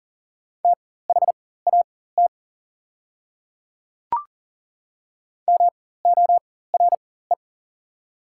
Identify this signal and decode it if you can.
Morse code, heard as a keyed tone, about 40 wpm, spaced as for 19 wpm, 690 Hz. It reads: THAT MORE
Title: 4x Character Spacing - Top 100 Words - 40wpm